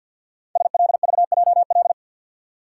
Morse code, heard as a keyed tone, about 52 wpm, 700 Hz. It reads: SLVJL
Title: Code Groups - 5 Letters - 50wpm